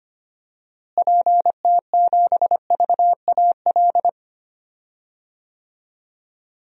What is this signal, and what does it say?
Morse code, heard as a keyed tone, about 25 wpm, 700 Hz. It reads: PT7VAL
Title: DXCC-100 Call Signs - 6 Characters - Repeated 3x - 25wpm